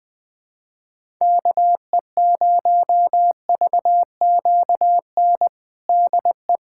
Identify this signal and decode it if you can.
Morse code, heard as a keyed tone, about 20 wpm, 700 Hz. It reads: KE0VQN DE
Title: USA Callsign DE Callsign 20wpm